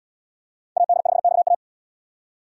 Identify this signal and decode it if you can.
Morse code, heard as a keyed tone, about 72 wpm, 690 Hz. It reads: AD5XA